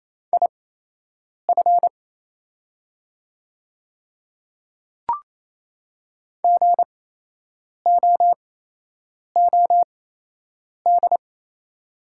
Morse code, heard as a keyed tone, about 28 wpm, spaced as for 8 wpm, 690 Hz. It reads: IF GOOD
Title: 8x Character Spacing - Top 200 Words - 28wpm